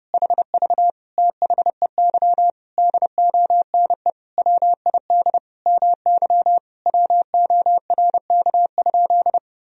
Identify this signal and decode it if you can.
Morse code, heard as a keyed tone, about 30 wpm, 700 Hz. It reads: HV THEY DONE WID MY WORK?
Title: Sentences from Top 500 Words and Abbreviations 30wpm